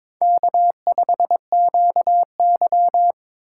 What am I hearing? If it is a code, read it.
Morse code, heard as a keyed tone, about 22 wpm, 700 Hz. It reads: K5QY
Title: USA Call Signs - 4 Characters - 8x Character Spacing - 22wpm